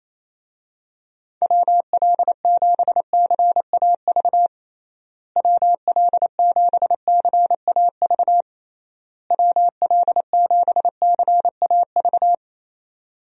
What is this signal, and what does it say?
Morse code, heard as a keyed tone, about 28 wpm, 700 Hz. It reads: WL7CAV WL7CAV WL7CAV
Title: USA Call Signs - 6 Characters - Repeated 3x - 28wpm